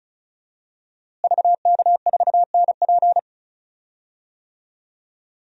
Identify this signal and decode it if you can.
Morse code, heard as a keyed tone, about 36 wpm, 700 Hz. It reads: VK4NP
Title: DXCC-100 Call Signs - 5 Characters - Repeated 3x - 35wpm